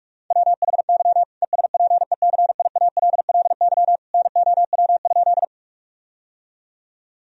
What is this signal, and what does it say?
Morse code, heard as a keyed tone, about 43 wpm, 700 Hz. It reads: WHY ESPECIALLY NOW?
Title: Sentences from Top 1000 Words - Review Part 4 - 45wpm